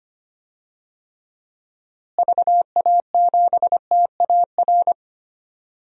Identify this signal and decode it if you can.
Morse code, heard as a keyed tone, about 25 wpm, 700 Hz. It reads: VA7TAR
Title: DXCC-100 Call Signs - 6 Characters - Repeated 3x - 25wpm